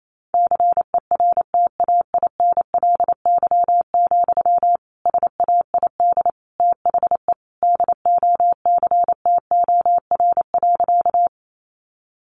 Morse code, heard as a keyed tone, about 28 wpm, 700 Hz. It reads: CERTAINLY, SAID THE DOCTOR.
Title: Sentences from Top 1500 Words 28wpm